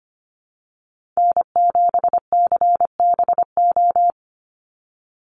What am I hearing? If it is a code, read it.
Morse code, heard as a keyed tone, about 25 wpm, 700 Hz. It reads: N7CBO
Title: USA Call Signs 25wpm